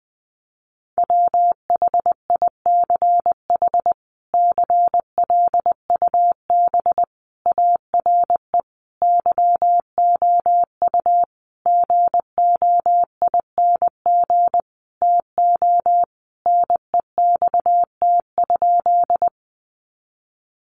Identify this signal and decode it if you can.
Morse code, heard as a keyed tone, about 19 wpm, 700 Hz. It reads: WHICH CLUB ARE YOU GOING TO NEXT?
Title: Sentences from Top 1200 Words 20wpm